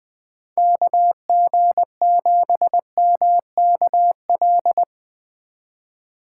Morse code, heard as a keyed tone, about 20 wpm, 700 Hz. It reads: KG7MKL